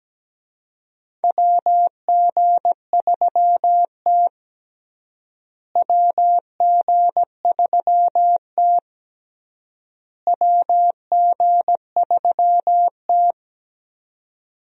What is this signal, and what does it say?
Morse code, heard as a keyed tone, about 17 wpm, 700 Hz. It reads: WG3T WG3T WG3T
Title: USA Call Signs - 4 Characters - Repeated 3x - 17wpm